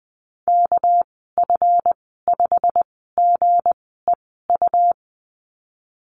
Morse code, heard as a keyed tone, about 20 wpm, spaced as for 15 wpm, 700 Hz. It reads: KF5GEU
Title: 207 Lesson 33g - Letter j - Call Signs 20wpm